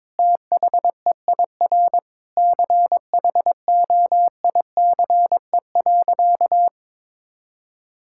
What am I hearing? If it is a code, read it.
Morse code, heard as a keyed tone, about 21 wpm, 700 Hz. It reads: THEIR CHOICE.
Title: Sentences from Top 1000 Words - Review Part 6 - 22wpm